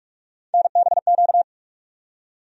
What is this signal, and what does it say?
Morse code, heard as a keyed tone, about 46 wpm, 700 Hz. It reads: NBQ
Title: USA Call Sign Postfixes - 3 Letters 45wpm